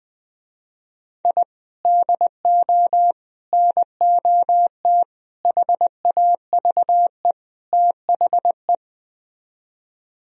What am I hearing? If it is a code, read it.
Morse code, heard as a keyed tone, about 20 wpm, 700 Hz. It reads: I DO NOT HAVE THE